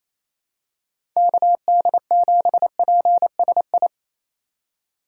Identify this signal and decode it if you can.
Morse code, heard as a keyed tone, about 28 wpm, 700 Hz. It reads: KD7PSI
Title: USA Call Signs - 6 Characters - 3x Character Spacing - 28wpm